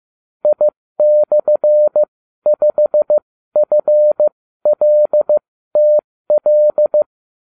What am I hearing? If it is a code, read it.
Morse code, heard as a keyed tone, about 15 wpm, spaced as for 13 wpm, 600 Hz. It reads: I/5FLTL